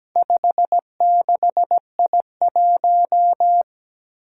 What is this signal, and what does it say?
Morse code, heard as a keyed tone, about 17 wpm, 700 Hz. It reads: 56I1
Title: Four Challenging Letter-Number Combinations 17wpm